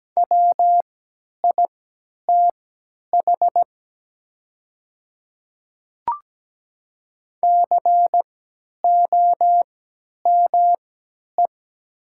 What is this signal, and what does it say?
Morse code, heard as a keyed tone, about 17 wpm, spaced as for 10 wpm, 700 Hz. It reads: WITH COME
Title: 3x Character Spacing - Top 100 Words - 17wpm